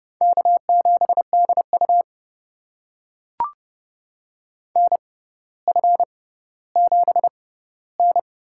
Morse code, heard as a keyed tone, about 30 wpm, spaced as for 15 wpm, 700 Hz. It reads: K7DU NF7N
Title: USA Call Signs - 4 Characters - 6x Character Spacing - 30wpm